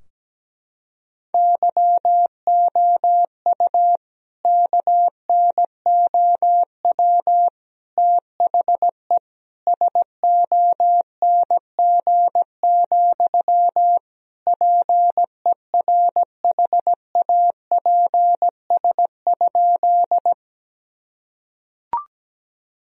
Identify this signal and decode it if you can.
Morse code, heard as a keyed tone, about 17 wpm, 700 Hz. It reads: YOU KNOW THE SONG, PERHAPS?
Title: Sentences from Top 1000 Words - Review Part 2 - 17wpm